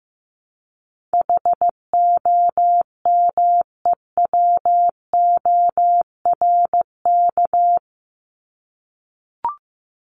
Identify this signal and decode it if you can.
Morse code, heard as a keyed tone, about 15 wpm, 700 Hz. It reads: HOMEWORK
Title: Top 1501-1750 Words 15wpm